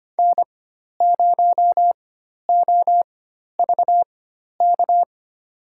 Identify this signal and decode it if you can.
Morse code, heard as a keyed tone, about 25 wpm, spaced as for 12 wpm, 700 Hz. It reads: N0OVK
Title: USA Call Signs - 5 Characters - 4x Character Spacing - 25wpm